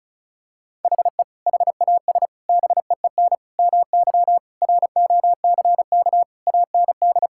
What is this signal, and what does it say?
Morse code, heard as a keyed tone, about 36 wpm, 700 Hz. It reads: HE HAS BEEN MY ROCK AND
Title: Sentences from Top 1000 Words - Review Part 5 - 35wpm